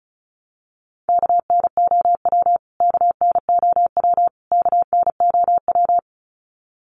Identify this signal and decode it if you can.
Morse code, heard as a keyed tone, about 34 wpm, 700 Hz. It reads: KNOW KNOW KNOW
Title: Top 300 Words Repeated 35wpm